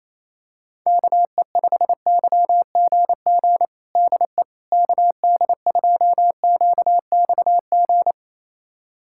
Morse code, heard as a keyed tone, about 28 wpm, 700 Hz. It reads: KE5YGG DE KD2QXG
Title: USA Callsign DE Callsign - 6 Characters - 28wpm